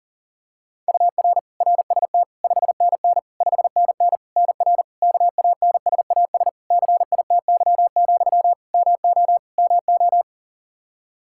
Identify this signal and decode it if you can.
Morse code, heard as a keyed tone, about 40 wpm, 700 Hz. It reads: UR RST 5NN 5NN NR KANSAS CITY, MO MO
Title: Practice QSO - Exchange 1 - Beginning - 40wpm